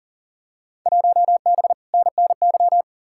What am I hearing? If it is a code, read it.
Morse code, heard as a keyed tone, about 40 wpm, 700 Hz. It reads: 1B NNY
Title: Field Day Exchanges Encore - 40wpm